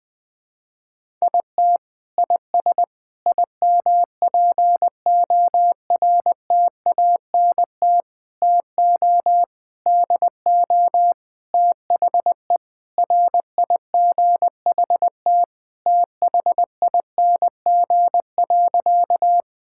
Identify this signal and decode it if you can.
Morse code, heard as a keyed tone, about 20 wpm, 700 Hz. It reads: IT IS IMPORTANT TO DO THE RIGHT THING.
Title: Sentences with 5-Word Phrases - 20wpm